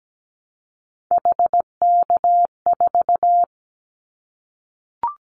Morse code, heard as a keyed tone, about 17 wpm, 700 Hz. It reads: HK4
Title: Callsign Prefixes for Top 100 DXCC - 17wpm